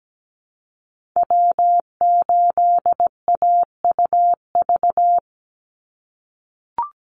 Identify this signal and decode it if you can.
Morse code, heard as a keyed tone, about 17 wpm, 700 Hz. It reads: W8AUV
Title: USA Call Signs 17wpm